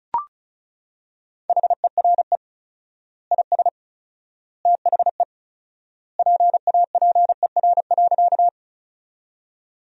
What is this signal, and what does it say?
Morse code, heard as a keyed tone, about 34 wpm, 700 Hz. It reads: HERE IS THE PAPER.
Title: 4x Word-Spacing - Sentences from Top 300 Words - 35wpm